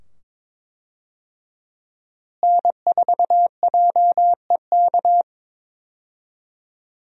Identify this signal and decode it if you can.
Morse code, heard as a keyed tone, about 22 wpm, 700 Hz. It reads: N4JEK
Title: USA Call Signs - 5 Characters - Repeated 2x - 22wpm